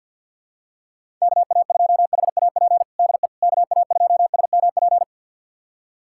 Morse code, heard as a keyed tone, about 52 wpm, 700 Hz. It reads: KA2HAP DE KA2SMP
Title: USA Callsign DE Callsign - 6 Characters - 50wpm